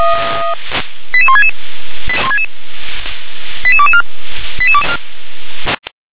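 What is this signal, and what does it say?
Five-tone selective call (ZVEI), heard as quick runs of five tones, tones of about 70 ms. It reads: 88166 88224
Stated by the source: ZVEI five-tone selective call 88187, received on SDR.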